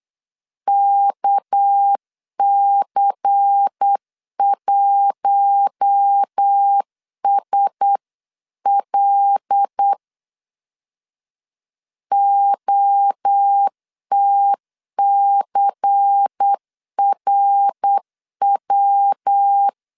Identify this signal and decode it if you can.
Morse code, heard as a keyed tone, about 8 wpm, 800 Hz. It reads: KC1SL OTCRW
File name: CW / DET / 1 6 1500C S B 8wpm 5142